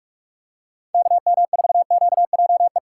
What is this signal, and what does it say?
Morse code, heard as a keyed tone, about 45 wpm, 700 Hz. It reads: KM4QJE
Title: USA Call Signs - 6 Characters - 4x Character Spacing - 45wpm